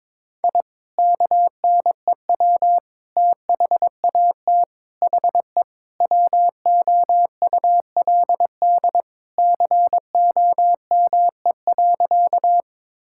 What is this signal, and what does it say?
Morse code, heard as a keyed tone, about 22 wpm, 700 Hz. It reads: I KNEW THAT HE WOULD COME.
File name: practice / 22wpm / Sentences with 2-Word Phrases - 22wpm